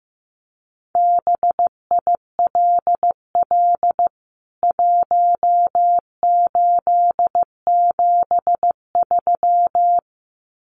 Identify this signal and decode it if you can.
Morse code, heard as a keyed tone, about 15 wpm, 700 Hz. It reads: BILL 1873
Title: CWT Exchange 15wpm